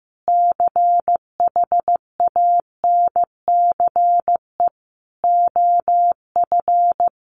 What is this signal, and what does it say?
Morse code, heard as a keyed tone, about 15 wpm, 700 Hz. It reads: CHANCE OF
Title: Sentences from Top 1300 Words 15wpm